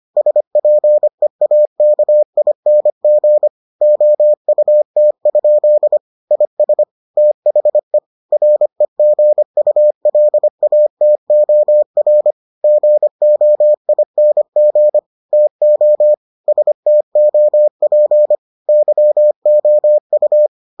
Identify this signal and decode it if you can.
Morse code, heard as a keyed tone, about 25 wpm, 600 Hz. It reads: SPEAKING OUT? IS THE REGULATOR GOING TO STOP YOU